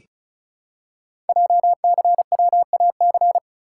Morse code, heard as a keyed tone, about 36 wpm, 700 Hz. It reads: JCWAC
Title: Code Groups - 5 Letters - 35wpm